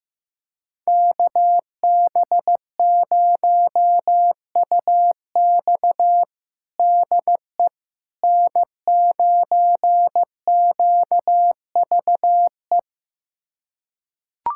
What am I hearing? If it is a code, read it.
Morse code, heard as a keyed tone, about 16 wpm, 700 Hz. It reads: KB0UX DE N9QVE
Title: USA Callsign DE Callsign 15wpm